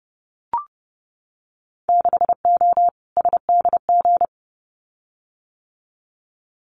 Morse code, heard as a keyed tone, about 30 wpm, 700 Hz. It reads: 6O SDG
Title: Winter Field Day Exchanges Encore 30wpm